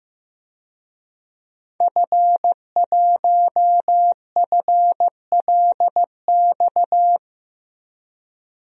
Morse code, heard as a keyed tone, about 16 wpm, 700 Hz. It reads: F1FLX
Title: DXCC-100 Call Signs - 5 Characters - Repeated 3x - 15wpm